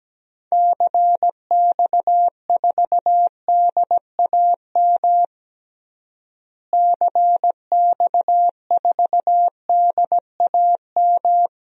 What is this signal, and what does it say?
Morse code, heard as a keyed tone, about 17 wpm, 700 Hz. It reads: CX4DAM CX4DAM
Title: DXCC-100 Call Signs - 6 Characters - Repeated 2x - 17wpm